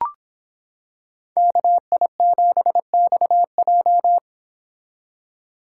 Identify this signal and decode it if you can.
Morse code, heard as a keyed tone, about 25 wpm, 700 Hz. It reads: KI7XJ